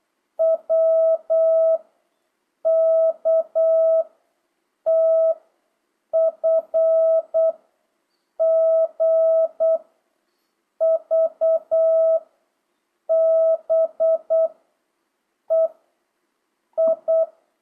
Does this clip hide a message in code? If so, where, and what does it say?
Yes: Morse code from 0.4 to 17.2 s, heard as a keyed tone, about 8 wpm, 660 Hz. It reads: W K T F G V B E I